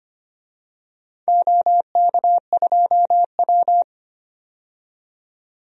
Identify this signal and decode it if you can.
Morse code, heard as a keyed tone, about 25 wpm, 700 Hz. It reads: OK2W
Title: DXCC-100 Call Signs - 4 Characters - 25wpm